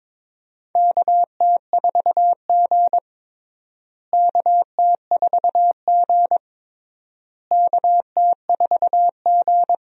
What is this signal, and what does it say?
Morse code, heard as a keyed tone, about 22 wpm, 700 Hz. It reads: KT4G KT4G KT4G
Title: USA Call Signs - 4 Characters - Repeated 3x - 22wpm